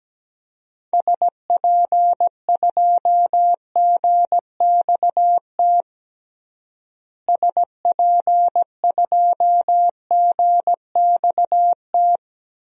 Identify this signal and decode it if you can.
Morse code, heard as a keyed tone, about 17 wpm, 700 Hz. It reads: SP2GXT SP2GXT
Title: DXCC-100 Call Signs - 6 Characters - Repeated 2x - 17wpm